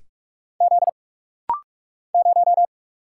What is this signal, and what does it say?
Morse code, heard as a keyed tone, about 44 wpm, 700 Hz. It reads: Z 0
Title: Single Letter-Number Alt - Rapid-Fire 45wpm